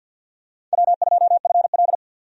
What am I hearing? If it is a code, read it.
Morse code, heard as a keyed tone, about 46 wpm, 700 Hz. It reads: W2FL
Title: USA Call Signs - 4 Characters - 50wpm